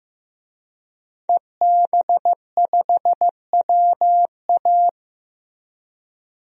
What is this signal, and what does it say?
Morse code, heard as a keyed tone, about 15 wpm, 700 Hz. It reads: EB5WA
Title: DXCC-100 Call Signs - 5 Characters - Repeated 3x - 15wpm